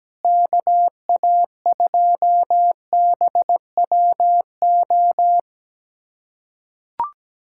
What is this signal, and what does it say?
Morse code, heard as a keyed tone, about 17 wpm, 700 Hz. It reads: KA2BWO
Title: USA Callsign DE Callsign - 6 Characters - 17wpm